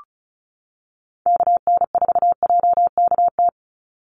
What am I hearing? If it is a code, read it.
Morse code, heard as a keyed tone, about 34 wpm, 700 Hz. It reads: KN4JKT
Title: USA Call Signs 35wpm